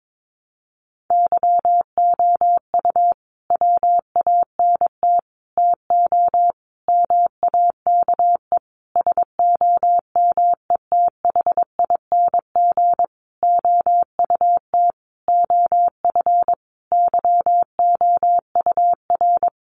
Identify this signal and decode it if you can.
Morse code, heard as a keyed tone, about 22 wpm, 700 Hz. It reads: YOU WANT TO MAKE SOMETHING OUT OF YOUR
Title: Sentences from Top 600 Words 22wpm